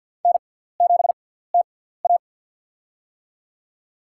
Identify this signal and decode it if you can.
Morse code, heard as a keyed tone, about 53 wpm, spaced as for 17 wpm, 700 Hz. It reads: N7TA
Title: USA Call Signs - 4 Characters - 6x Character Spacing - 50wpm